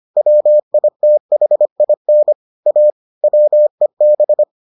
Morse code, heard as a keyed tone, about 25 wpm, 600 Hz. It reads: WITHIN A WEB